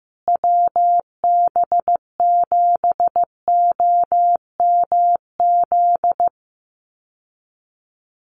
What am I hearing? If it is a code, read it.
Morse code, heard as a keyed tone, about 15 wpm, 700 Hz. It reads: WB7OMZ
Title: USA Call Signs Repeated 2x 15wpm